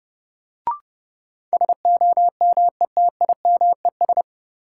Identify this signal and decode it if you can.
Morse code, heard as a keyed tone, about 30 wpm, 700 Hz. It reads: SOMETIMES
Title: Sentences from Top 600 Words - New Word Drill - 30wpm